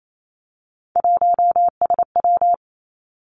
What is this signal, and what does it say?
Morse code, heard as a keyed tone, about 28 wpm, 700 Hz. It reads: 1SW
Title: Three Challenging Letter-Number Combinations 28wpm